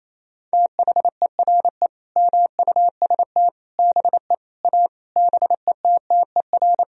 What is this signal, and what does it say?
Morse code, heard as a keyed tone, about 28 wpm, 700 Hz. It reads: THERE MUST BE A BETTER
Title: Sentences from Top 1000 Words - Review Part 4 - 28wpm